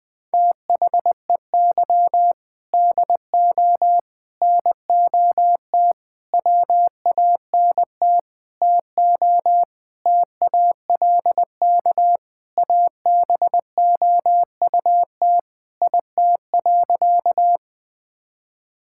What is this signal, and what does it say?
Morse code, heard as a keyed tone, about 20 wpm, 700 Hz. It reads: THEY DO NOT WANT TO TALK ABOUT IT.
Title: Sentences with 5-Word Phrases - 20wpm